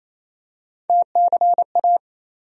Morse code, heard as a keyed tone, about 28 wpm, 700 Hz. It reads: TCA
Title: USA Call Sign Postfixes - 3 Letters 28wpm